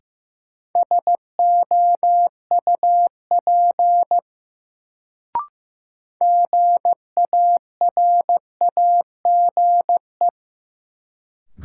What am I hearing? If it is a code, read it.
Morse code, heard as a keyed tone, about 15 wpm, 700 Hz. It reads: SOUP GARAGE
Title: Sentences from Top 1600 Words - New Word Drill Alt - 15wpm